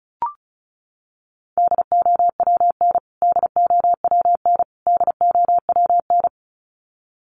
Morse code, heard as a keyed tone, about 34 wpm, 700 Hz. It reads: DOWN DOWN DOWN
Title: Top 200 Words Repeated 35wpm